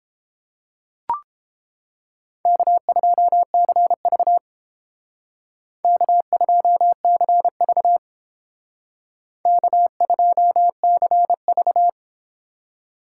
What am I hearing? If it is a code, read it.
Morse code, heard as a keyed tone, about 30 wpm, 700 Hz. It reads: K2CV K2CV K2CV